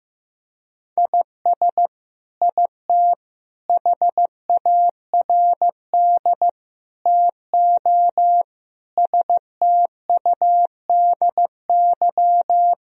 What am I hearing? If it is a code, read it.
Morse code, heard as a keyed tone, about 15 wpm, 700 Hz. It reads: IS IT HARD TO STUDY
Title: Sentences from Top 1000 Words - Review Part 3 - 15wpm